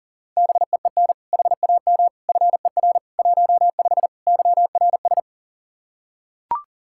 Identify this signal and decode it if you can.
Morse code, heard as a keyed tone, about 37 wpm, 700 Hz. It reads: BEEN HAM FER 15 YRS
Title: Practice QSO - Exchange 3 - More Info - 40wpm